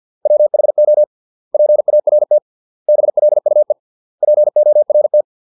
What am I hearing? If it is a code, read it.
Morse code, heard as a keyed tone, about 50 wpm, 600 Hz. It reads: WHO PART BLUE PORT